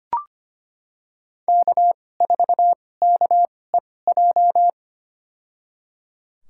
Morse code, heard as a keyed tone, about 25 wpm, spaced as for 18 wpm, 700 Hz. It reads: K4KEJ
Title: USA Call Signs - 5 Characters - 2x Character Spacing - 25wpm